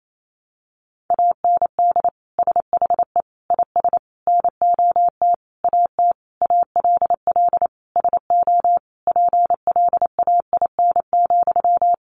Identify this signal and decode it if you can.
Morse code, heard as a keyed tone, about 28 wpm, 700 Hz. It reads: AND SHE IS NOT AT ALL SO PLAIN,
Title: Sentences from Top 600 Words 28wpm